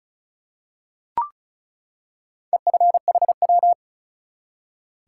Morse code, heard as a keyed tone, about 36 wpm, 700 Hz. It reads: EFHW
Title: Top 180 QSO Elements 35wpm